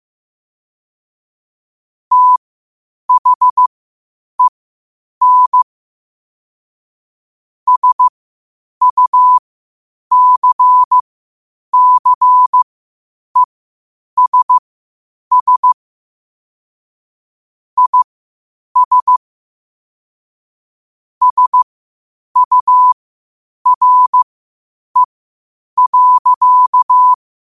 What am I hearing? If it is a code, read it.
Morse code, heard as a keyed tone, about 15 wpm, spaced as for 8 wpm, 1000 Hz. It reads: THEN SUCCESS IS SURE.
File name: qotdfiles / 8wpm